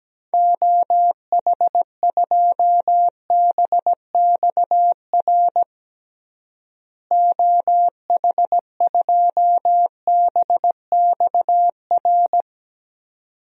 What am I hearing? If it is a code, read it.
Morse code, heard as a keyed tone, about 17 wpm, 700 Hz. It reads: OH2BXR OH2BXR
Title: DXCC-100 Call Signs - 6 Characters - Repeated 2x - 17wpm